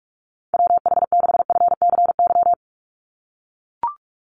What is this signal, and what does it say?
Morse code, heard as a keyed tone, about 44 wpm, 700 Hz. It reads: WH6FCY
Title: USA Call Signs 45wpm